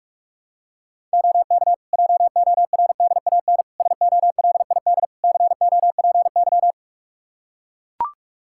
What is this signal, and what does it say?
Morse code, heard as a keyed tone, about 42 wpm, 700 Hz. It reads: OK JORDAN SOLID COPY
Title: Practice QSO - Exchange 3 - More Info - 45wpm